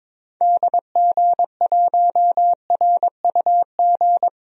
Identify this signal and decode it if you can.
Morse code, heard as a keyed tone, about 22 wpm, 700 Hz. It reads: DG1RUG
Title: DXCC-100 Call Signs - 6 Characters - 22wpm